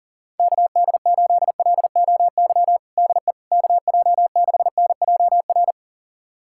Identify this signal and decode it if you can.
Morse code, heard as a keyed tone, about 40 wpm, 700 Hz. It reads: KD8LOY DE KJ6NJR
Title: USA Callsign DE Callsign - 6 Characters - 40wpm